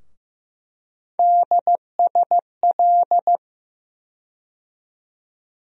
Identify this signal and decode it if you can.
Morse code, heard as a keyed tone, about 15 wpm, 700 Hz. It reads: DSL
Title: USA Call Sign Postfixes - 3 Letters 15wpm